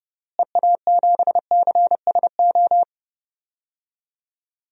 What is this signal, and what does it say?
Morse code, heard as a keyed tone, about 31 wpm, 700 Hz. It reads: EA7CSO
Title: DXCC-100 Call Signs - 6 Characters - Repeated 3x - 30wpm